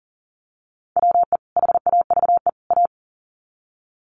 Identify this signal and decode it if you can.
Morse code, heard as a keyed tone, about 40 wpm, 700 Hz. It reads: WE HAVE A
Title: Sets of 3 Words 40wpm